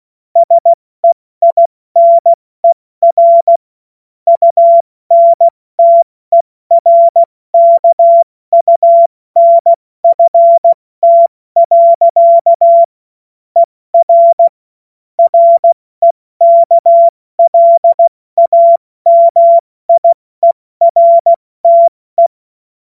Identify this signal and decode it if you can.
Morse code, heard as a keyed tone, about 16 wpm, 680 Hz. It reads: SEINER UNTERKUNFT. ER REKLAMIERTE
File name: Morsekurs / Witze / Witz 30. August 2022 70 F80